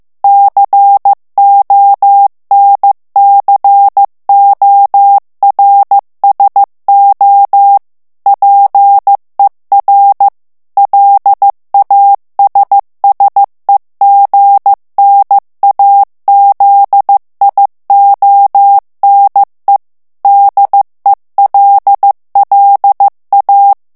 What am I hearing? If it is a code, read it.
Morse code, heard as a keyed tone, about 15 wpm, 800 Hz. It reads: CONCORSO PER LASSEGNAZIONE DELLA
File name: morse mp3 / ITALIEN texte clair 2 vitesse 14 mots minute